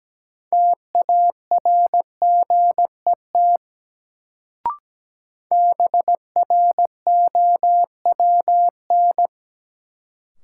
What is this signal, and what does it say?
Morse code, heard as a keyed tone, about 17 wpm, 700 Hz. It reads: TARGET BROWN
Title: Sentences from Top 1200 Words - New Word Drill Alt - 17wpm